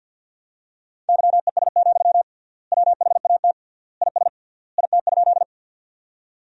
Morse code, heard as a keyed tone, about 46 wpm, 700 Hz. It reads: YES, WHAT IS IT?
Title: 3x Word-Spacing - Sentences from Top 500 Words - 50wpm